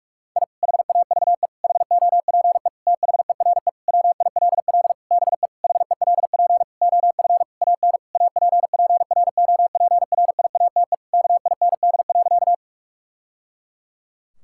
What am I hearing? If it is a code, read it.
Morse code, heard as a keyed tone, about 41 wpm, 700 Hz. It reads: I HAVE HOPE THERE WILL BE HELP OF AN APPROPRIATE KIND.
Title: Sentences from Top 1000 Words - Review Part 5 - 45wpm